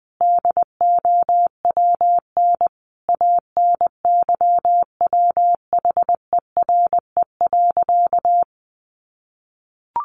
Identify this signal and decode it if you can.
Morse code, heard as a keyed tone, about 20 wpm, 700 Hz. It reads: DOWN ANYWHERE.
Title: Sentences from Top 1400 Words 20wpm